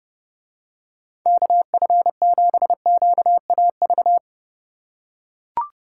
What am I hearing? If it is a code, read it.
Morse code, heard as a keyed tone, about 29 wpm, 700 Hz. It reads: KF7QAV